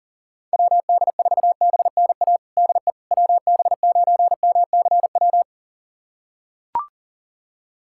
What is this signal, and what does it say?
Morse code, heard as a keyed tone, about 37 wpm, 700 Hz. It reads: WD4BNA DE WB9MCW
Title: USA Callsign DE Callsign - 6 Characters - 40wpm